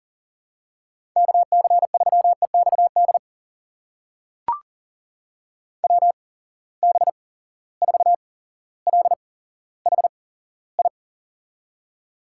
Morse code, heard as a keyed tone, about 40 wpm, spaced as for 17 wpm, 700 Hz. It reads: KC3EXD WB4LHI
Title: USA Call Signs - 6 Characters - 8x Character Spacing - 40wpm